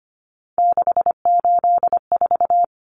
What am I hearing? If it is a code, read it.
Morse code, heard as a keyed tone, about 25 wpm, 700 Hz. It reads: 684
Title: Numbers - 3-Digit 25wpm